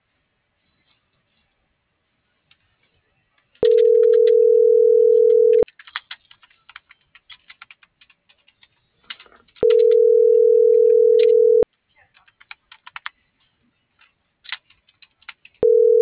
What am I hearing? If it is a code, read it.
Ringback tone (North America), 440 + 480 Hz.
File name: CENTRAL MAYAB / out fernanda.pamela